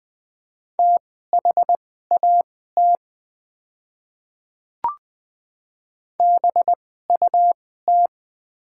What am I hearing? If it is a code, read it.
Morse code, heard as a keyed tone, about 20 wpm, spaced as for 14 wpm, 700 Hz. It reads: THAT BUT
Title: 2x Character Spacing - Top 200 Words - 20wpm